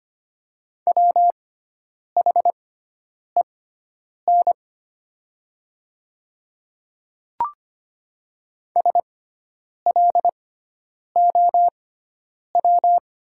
Morse code, heard as a keyed tone, about 25 wpm, spaced as for 9 wpm, 700 Hz. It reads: WHEN SLOW
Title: 6x Character Spacing - Top 500 Words - 25wpm